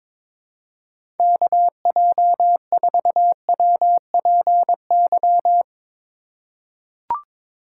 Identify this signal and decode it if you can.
Morse code, heard as a keyed tone, about 22 wpm, 700 Hz. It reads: KJ4WPY
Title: USA Call Signs - 6 Characters - 3x Character Spacing - 22wpm